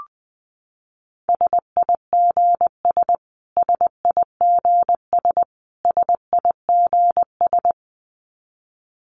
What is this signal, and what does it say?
Morse code, heard as a keyed tone, about 19 wpm, 700 Hz. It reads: SIGS SIGS SIGS
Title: Abbreviations Repeated 3x - 20wpm